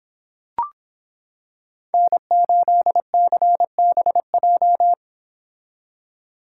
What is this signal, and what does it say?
Morse code, heard as a keyed tone, about 26 wpm, 700 Hz. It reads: N8CBJ